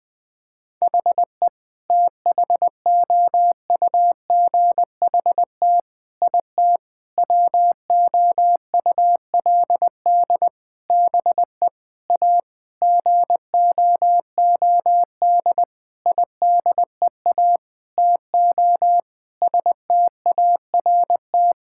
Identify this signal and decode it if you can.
Morse code, heard as a keyed tone, about 20 wpm, 700 Hz. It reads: HE THOUGHT IT WOULD BE A GOOD IDEA TO START
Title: Sentences with 8-Word Phrases - 20wpm